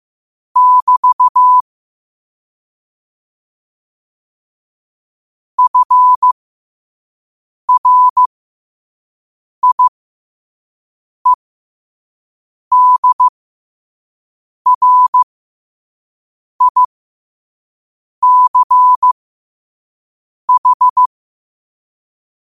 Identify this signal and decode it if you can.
Morse code, heard as a keyed tone, about 15 wpm, spaced as for 5 wpm, 1000 Hz. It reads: <BT> FRIEDRICH
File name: qotdfiles / 5wpm